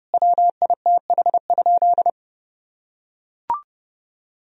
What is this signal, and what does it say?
Morse code, heard as a keyed tone, about 30 wpm, 700 Hz. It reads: WITH?
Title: Sentences from Top 1000 Words - Review Part 1 - 30wpm